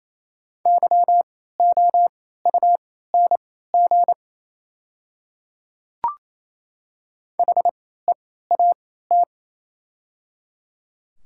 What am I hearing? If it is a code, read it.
Morse code, heard as a keyed tone, about 28 wpm, spaced as for 16 wpm, 690 Hz. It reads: YOUNG HEAT
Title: 3x Character Spacing - Top 500 Words - 28wpm